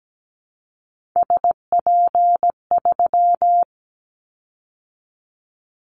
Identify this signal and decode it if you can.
Morse code, heard as a keyed tone, about 17 wpm, 700 Hz. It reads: SP3
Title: Callsign Prefixes for Top 25 DXCC - 17wpm